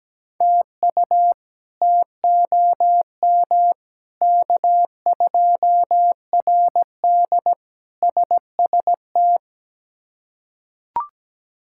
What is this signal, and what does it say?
Morse code, heard as a keyed tone, about 17 wpm, 700 Hz. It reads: TU TOM K2RD SST
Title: K1USN SST QSO Practice - 17wpm